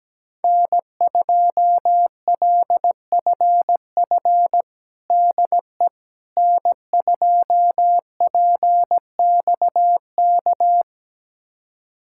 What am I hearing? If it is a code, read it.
Morse code, heard as a keyed tone, about 17 wpm, 700 Hz. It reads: N2LFF DE N2PXK
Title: USA Callsign DE Callsign - 5 Characters - 17wpm